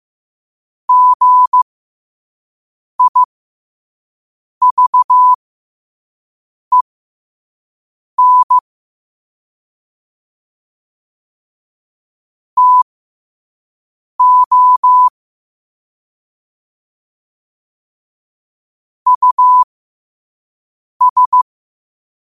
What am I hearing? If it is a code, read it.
Morse code, heard as a keyed tone, about 15 wpm, spaced as for 5 wpm, 1000 Hz. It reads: GIVEN TO US
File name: qotdfiles / 5wpm